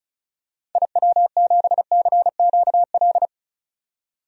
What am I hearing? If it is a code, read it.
Morse code, heard as a keyed tone, about 34 wpm, 700 Hz. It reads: IW7CQL
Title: DXCC-100 Call Signs - 6 Characters - 35wpm